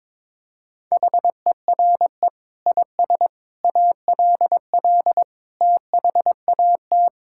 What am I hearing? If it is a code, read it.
Morse code, heard as a keyed tone, about 22 wpm, 700 Hz. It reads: HERE IS ALL THAT
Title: Sentences from Top 1000 Words - Review Part 6 - 22wpm